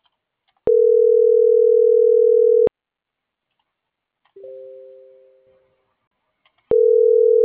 Ringback tone (North America), 440 + 480 Hz.